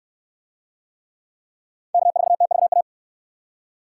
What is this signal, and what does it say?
Morse code, heard as a keyed tone, about 69 wpm, 700 Hz. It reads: K4TVA